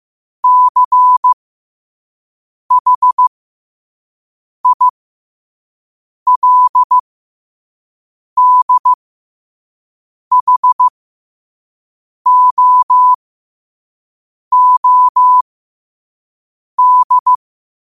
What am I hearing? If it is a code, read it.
Morse code, heard as a keyed tone, about 15 wpm, spaced as for 5 wpm, 1000 Hz. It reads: CHILDHOOD
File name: qotdfiles / 5wpm